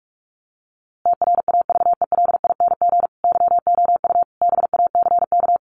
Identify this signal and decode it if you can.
Morse code, heard as a keyed tone, about 44 wpm, 700 Hz. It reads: TRAVELING YOU BACK
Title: A Sound Of Thunder 45wpm